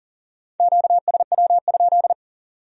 Morse code, heard as a keyed tone, about 44 wpm, 700 Hz. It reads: QSW?
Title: Speed-Racing Q-Codes - 40wpm